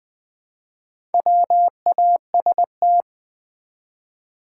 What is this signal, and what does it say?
Morse code, heard as a keyed tone, about 20 wpm, 700 Hz. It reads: WAST
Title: Build A Word - 1 Letter At A Time - 5-Letter Words - 20wpm